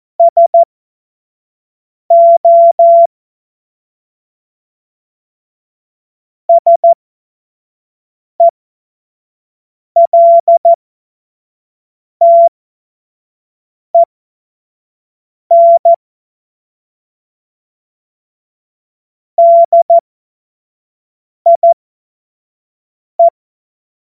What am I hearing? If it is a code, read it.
Morse code, heard as a keyed tone, about 14 wpm, spaced as for 5 wpm, 680 Hz. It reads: SO SELTEN DIE
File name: Morsekurs / Witze / Witz 16. Mai 2023 25 F70